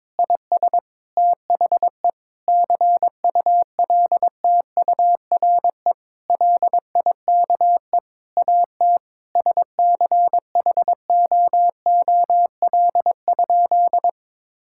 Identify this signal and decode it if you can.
Morse code, heard as a keyed tone, about 22 wpm, 700 Hz. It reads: IS THE CULTURE LIKE AT SCHOOL?
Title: Sentences from Top 1000 Words - Review Part 6 - 22wpm